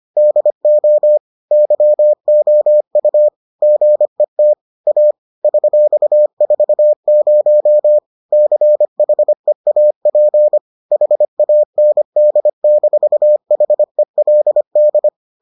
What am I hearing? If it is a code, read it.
Morse code, heard as a keyed tone, about 25 wpm, 600 Hz. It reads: DO YOU GET A $40 CHEAP HAND-HELD